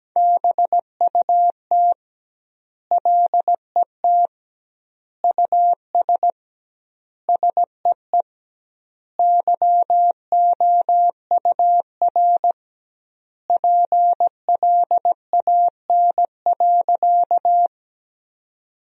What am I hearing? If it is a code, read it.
Morse code, heard as a keyed tone, about 17 wpm, 700 Hz. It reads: BUT LET US SEE YOUR PLAN.
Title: 2x Word-Spacing - Sentences from Top 500 Words - 17wpm